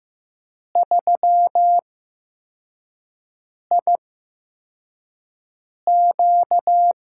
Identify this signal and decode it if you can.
Morse code, heard as a keyed tone, about 15 wpm, spaced as for 4 wpm, 700 Hz. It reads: 3IQ
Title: USA Call Signs - 4 Characters - 8x Character Spacing - 15wpm